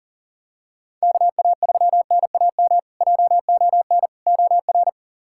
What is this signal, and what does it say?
Morse code, heard as a keyed tone, about 40 wpm, 700 Hz. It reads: KA3NAM JON OR
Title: K1USN SST QSO Practice - 40wpm